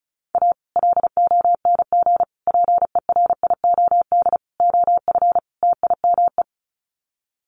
Morse code, heard as a keyed tone, about 34 wpm, 700 Hz. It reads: A LONG PERIOD OF TIME
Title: Sets of 5 Words 35wpm